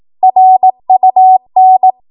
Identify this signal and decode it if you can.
Morse code, heard as a keyed tone, about 18 wpm, 750 Hz. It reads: RUN
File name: Archive / 18 WPM / 18WPM